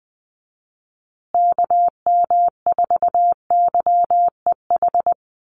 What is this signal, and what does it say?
Morse code, heard as a keyed tone, about 20 wpm, 700 Hz. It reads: KM4YEH
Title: USA Call Signs Repeated 2x 20wpm